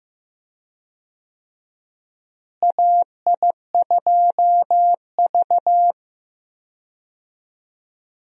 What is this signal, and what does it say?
Morse code, heard as a keyed tone, about 15 wpm, 700 Hz. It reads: AI2V
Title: USA Call Signs - 4 Characters - Repeated 3x - 15wpm